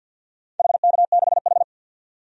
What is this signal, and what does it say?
Morse code, heard as a keyed tone, about 50 wpm, 700 Hz. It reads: HK6H
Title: DXCC-100 Call Signs - 4 Characters - 50wpm